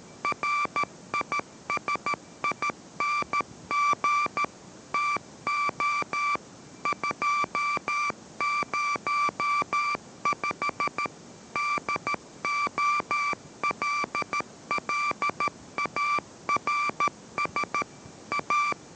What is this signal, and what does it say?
Morse code, heard as a keyed tone, about 13 wpm, 1170 Hz, with noise. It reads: RISING TO 205 DOLLARS A